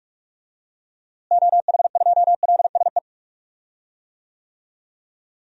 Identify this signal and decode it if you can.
Morse code, heard as a keyed tone, about 41 wpm, 700 Hz. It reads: OH2LSE
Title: DXCC-100 Call Signs - 6 Characters - Repeated 3x - 45wpm